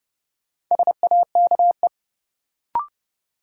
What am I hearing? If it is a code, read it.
Morse code, heard as a keyed tone, about 30 wpm, 700 Hz. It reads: SAKE
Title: Sets of 4 Words - New Word Drill - 30wpm